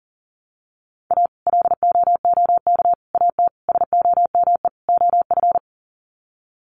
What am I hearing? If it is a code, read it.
Morse code, heard as a keyed tone, about 40 wpm, 700 Hz. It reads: A LOOK AT SOME OF
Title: Sets of 5 Words Encore2 40wpm